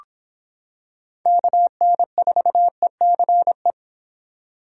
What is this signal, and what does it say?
Morse code, heard as a keyed tone, about 26 wpm, 700 Hz. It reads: KN4ECE